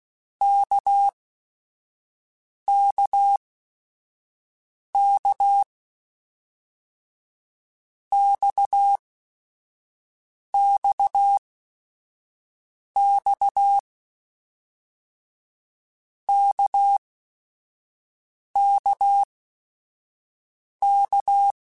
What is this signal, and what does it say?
Morse code, heard as a keyed tone, about 16 wpm, spaced as for 5 wpm, 800 Hz. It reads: KKK XXX KKK
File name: Basic Sets / Basic Letter Groups Collective7 / RKPX@5